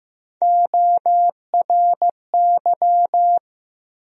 Morse code, heard as a keyed tone, about 15 wpm, 700 Hz. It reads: ORY
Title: USA Call Sign Postfixes - 3 Letters 15wpm